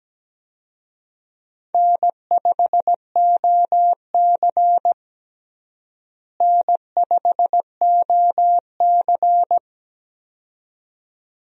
Morse code, heard as a keyed tone, about 17 wpm, 700 Hz. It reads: N5OC N5OC
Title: USA Call Signs - 4 Characters - Repeated 2x - 17wpm